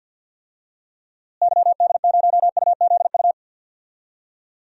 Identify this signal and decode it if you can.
Morse code, heard as a keyed tone, about 53 wpm, 700 Hz. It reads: YD0UZU
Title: DXCC-100 Call Signs - 6 Characters - Repeated 3x - 50wpm